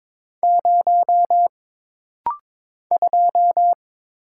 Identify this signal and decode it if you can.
Morse code, heard as a keyed tone, about 22 wpm, 700 Hz. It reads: 0 2
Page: This rapid-fire timing is geared to push your ICR (Instant Character Recognition) skills.